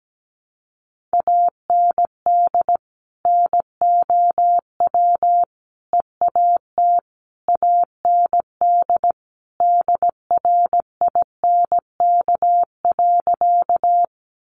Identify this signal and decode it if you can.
Morse code, heard as a keyed tone, about 17 wpm, 700 Hz. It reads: AND NOW EAT AND DRINK.
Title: Sentences from Top 1500 Words 17wpm